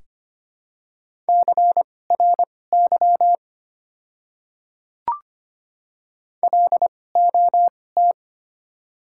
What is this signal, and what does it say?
Morse code, heard as a keyed tone, about 25 wpm, spaced as for 18 wpm, 700 Hz. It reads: CRY LOT
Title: 2x Character Spacing - Top 500 Words - 25wpm